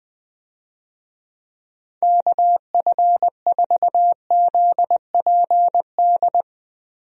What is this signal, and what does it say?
Morse code, heard as a keyed tone, about 20 wpm, 700 Hz. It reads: KF4ZPD
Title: USA Call Signs - 6 Characters - 20wpm